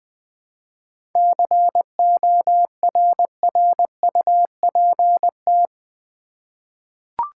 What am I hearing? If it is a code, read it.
Morse code, heard as a keyed tone, about 20 wpm, 700 Hz. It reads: CORRUPT
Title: 7 Letter Words - First 3 Letters Given - 20wpm